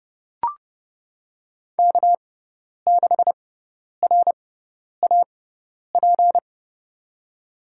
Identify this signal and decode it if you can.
Morse code, heard as a keyed tone, about 31 wpm, spaced as for 10 wpm, 700 Hz. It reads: K6RAP
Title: USA Call Signs - 5 Characters - 6x Character Spacing - 30wpm